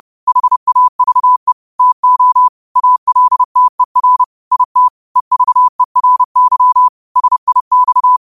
Morse code, heard as a keyed tone, about 30 wpm, 1000 Hz. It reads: HAVE TO ALTER IT EVERY SIX